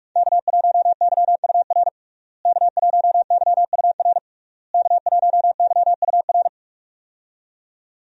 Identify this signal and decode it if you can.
Morse code, heard as a keyed tone, about 46 wpm, 700 Hz. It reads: K1YUR K1YUR K1YUR
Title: USA Call Signs - 5 Characters - Repeated 3x - 45wpm